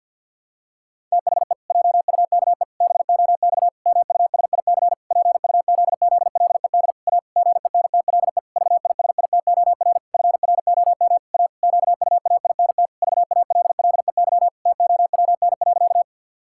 Morse code, heard as a keyed tone, about 51 wpm, 700 Hz. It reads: THE JUKE BOX MUSIC PUZZLED A GENTLE VISITOR FROM A QUAINT VALLEY TOWN.
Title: Pangram Challenge v2 - 50wpm